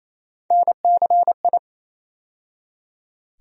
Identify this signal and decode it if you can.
Morse code, heard as a keyed tone, about 28 wpm, 700 Hz. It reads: NCI
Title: USA Call Sign Postfixes - 3 Letters 28wpm